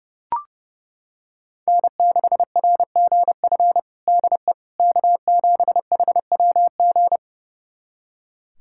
Morse code, heard as a keyed tone, about 30 wpm, 700 Hz. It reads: N6RGF DE K7HWG
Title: USA Callsign DE Callsign - 5 Characters - 30wpm